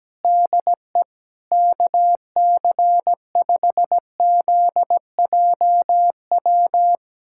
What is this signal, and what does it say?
Morse code, heard as a keyed tone, about 17 wpm, 700 Hz. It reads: DE KC5ZJW
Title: USA Callsign DE Callsign 17wpm